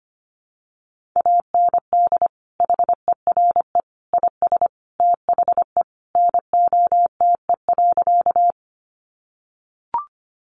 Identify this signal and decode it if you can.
Morse code, heard as a keyed tone, about 25 wpm, 700 Hz. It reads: AND HERE IS THE NOTE.
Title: Sentences from Top 500 Words 25wpm